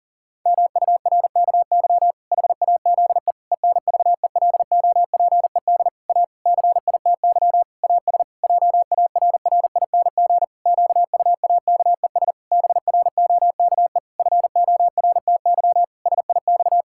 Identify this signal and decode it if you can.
Morse code, heard as a keyed tone, about 40 wpm, 700 Hz. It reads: MURKY HAZE ENVELOPED A CITY AS JARRING QUAKES BROKE FORTY SIX
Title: Pangram Challenge v2 - 40wpm